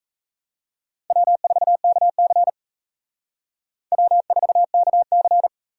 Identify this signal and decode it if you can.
Morse code, heard as a keyed tone, about 34 wpm, 700 Hz. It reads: W4KC W4KC
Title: Speed Racing - USA 1x2 Call Signs 28wpm